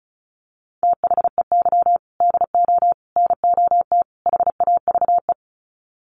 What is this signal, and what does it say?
Morse code, heard as a keyed tone, about 34 wpm, 700 Hz. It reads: THEY DO NOT HAVE
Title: Sets of 4 Words 35wpm